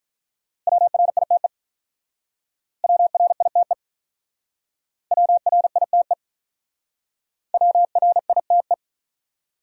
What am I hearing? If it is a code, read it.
Morse code, heard as a keyed tone, about 45 wpm, 700 Hz. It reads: WRITE WRITE WRITE WRITE